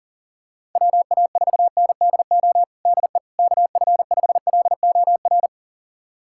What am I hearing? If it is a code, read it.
Morse code, heard as a keyed tone, about 40 wpm, 700 Hz. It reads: WA4NDO DE KF5LOR
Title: USA Callsign DE Callsign 40wpm